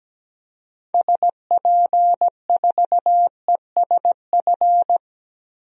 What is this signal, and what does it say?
Morse code, heard as a keyed tone, about 17 wpm, 700 Hz. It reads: SP4ESF
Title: DXCC-100 Call Signs - 6 Characters - Repeated 2x - 17wpm